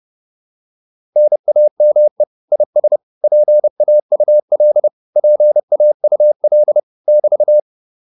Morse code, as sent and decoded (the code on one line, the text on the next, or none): -. .- -- . / .. ... / .--. .- ..- .-.. / .--. .- ..- .-.. / -...-
NAME IS PAUL PAUL <BT>